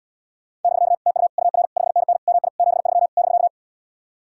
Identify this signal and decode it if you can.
Morse code, heard as a keyed tone, about 79 wpm, 700 Hz. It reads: <BK> TU UR 5NN MI 73 <BK>